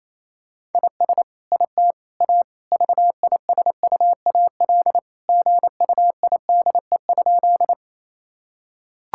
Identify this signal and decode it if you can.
Morse code, heard as a keyed tone, about 27 wpm, 700 Hz. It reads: IS IT A VISUAL GUIDE?
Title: Sentences from Top 1000 Words - Review Part 6 - 28wpm